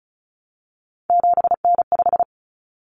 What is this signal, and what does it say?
Morse code, heard as a keyed tone, about 34 wpm, 700 Hz. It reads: 7N5
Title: Common Cut Numbers - 3-Digit 35wpm